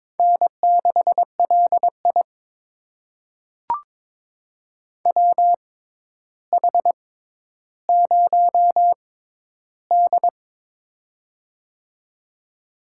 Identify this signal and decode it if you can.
Morse code, heard as a keyed tone, about 22 wpm, spaced as for 11 wpm, 700 Hz. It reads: N6LI WH0D
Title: USA Call Signs - 6x Character Spacing - 22wpm